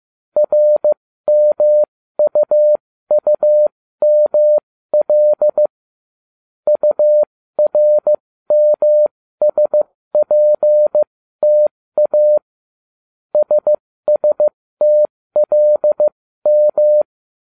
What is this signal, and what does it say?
Morse code, heard as a keyed tone, about 15 wpm, spaced as for 13 wpm, 600 Hz. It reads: RMUUML URMSPTA SSTLM